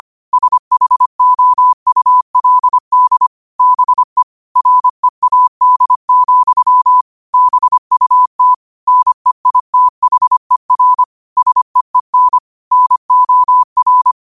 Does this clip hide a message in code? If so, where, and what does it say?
Yes: Morse code from 0.3 to 14.1 s, heard as a keyed tone, about 25 wpm, 1000 Hz. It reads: SHOULD BE READ, BUT NEITHER SEEN NOR